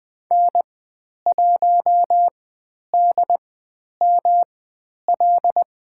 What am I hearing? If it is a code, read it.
Morse code, heard as a keyed tone, about 20 wpm, spaced as for 10 wpm, 700 Hz. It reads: N1DML
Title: 109 Lesson 18g - Letter m - Call Signs 20wpm